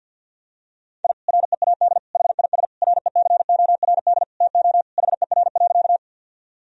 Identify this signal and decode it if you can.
Morse code, heard as a keyed tone, about 50 wpm, 700 Hz. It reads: I READ HIS RECORD TO HER.
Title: Sentences from Top 1000 Words - Review Part 3 - 50wpm